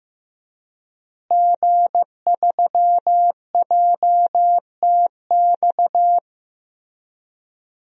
Morse code, heard as a keyed tone, about 15 wpm, 700 Hz. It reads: G3JTX
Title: DXCC-100 Call Signs - 5 Characters - Repeated 3x - 15wpm